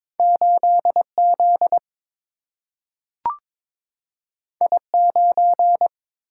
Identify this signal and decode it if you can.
Morse code, heard as a keyed tone, about 22 wpm, 700 Hz. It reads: 8Z I9
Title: Single-Letter Single-Number 22wpm